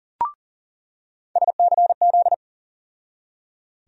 Morse code, heard as a keyed tone, about 38 wpm, 700 Hz. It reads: SCZ
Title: USA Call Sign Postfixes - 3 Letters 40wpm